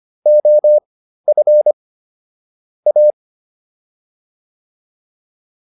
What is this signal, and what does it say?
Morse code, heard as a keyed tone, about 25 wpm, spaced as for 13 wpm, 600 Hz. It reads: OF A